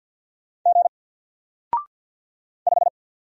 Morse code, heard as a keyed tone, about 50 wpm, 700 Hz. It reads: G 5
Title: Single Letter-Number 50wpm